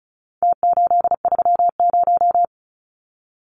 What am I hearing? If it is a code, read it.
Morse code, heard as a keyed tone, about 34 wpm, 700 Hz. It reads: T830
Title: Numbers - Contest Style - 4-Digit 35wpm